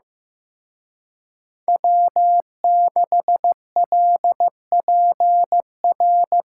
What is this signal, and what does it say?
Morse code, heard as a keyed tone, about 18 wpm, 700 Hz. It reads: W6LPR
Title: Speed Racing - USA 1x3 Call Signs 15wpm